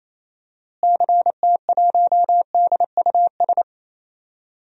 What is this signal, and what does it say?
Morse code, heard as a keyed tone, about 28 wpm, 700 Hz. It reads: CT1DUS
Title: DXCC-100 Call Signs - 6 Characters - Repeated 2x - 28wpm